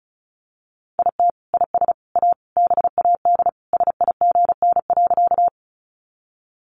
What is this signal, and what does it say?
Morse code, heard as a keyed tone, about 34 wpm, 700 Hz. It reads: IT IS A BAD SIGN.
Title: Sentences from Top 1000 Words 35wpm